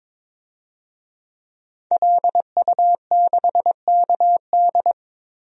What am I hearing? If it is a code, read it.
Morse code, heard as a keyed tone, about 22 wpm, 700 Hz. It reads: LU6KD
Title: DXCC-100 Call Signs - 5 Characters - Repeated 2x - 22wpm